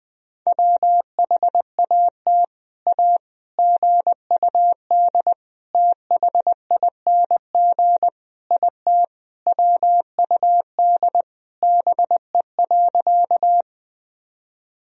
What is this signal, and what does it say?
Morse code, heard as a keyed tone, about 20 wpm, 700 Hz. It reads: WHAT A GUD THING IT WUD BE.
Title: Sentences from Top 200 Words and Abbreviations 20wpm